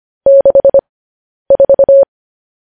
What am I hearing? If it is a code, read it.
Morse code, heard as a keyed tone, about 25 wpm, spaced as for 10 wpm, 550 Hz. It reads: 64
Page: news headlines in morse code at 10 wpm